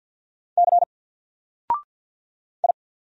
Morse code, heard as a keyed tone, about 50 wpm, 700 Hz. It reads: C I
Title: Single Character (Letter:Number:Punctuation:Prosign) 50wpm